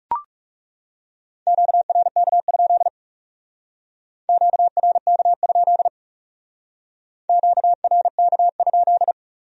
Morse code, heard as a keyed tone, about 36 wpm, 700 Hz. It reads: QRK? QRK? QRK?
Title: Speed-Racing Q-Codes - 30wpm